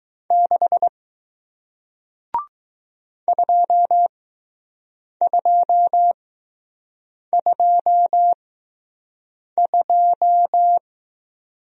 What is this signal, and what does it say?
Morse code, heard as a keyed tone, about 20 wpm, spaced as for 6 wpm, 700 Hz. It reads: 6 2222